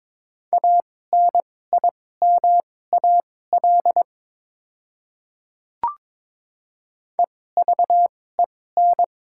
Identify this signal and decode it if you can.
Morse code, heard as a keyed tone, about 22 wpm, spaced as for 16 wpm, 700 Hz. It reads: ANIMAL EVEN
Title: 2x Character Spacing - Top 500 Words - 22wpm